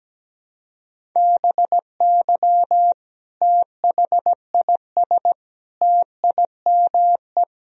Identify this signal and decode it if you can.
Morse code, heard as a keyed tone, about 17 wpm, 700 Hz. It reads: BY THIS TIME
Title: Sentences from Top 1000 Words - Review Part 1 - 17wpm